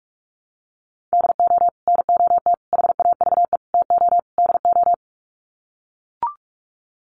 Morse code, heard as a keyed tone, about 44 wpm, 700 Hz. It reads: DO NOT HAVE TO DO
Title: Sets of 5 Words 45wpm